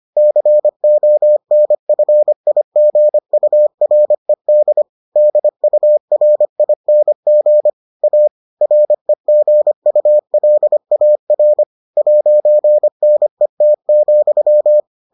Morse code, heard as a keyed tone, about 25 wpm, 600 Hz. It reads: CONFIGURED DURING A REGULAR 'NET,